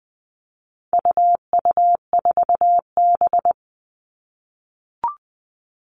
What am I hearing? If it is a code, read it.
Morse code, heard as a keyed tone, about 20 wpm, 700 Hz. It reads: UU4B
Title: All Cut Numbers - 4-Digit 20wpm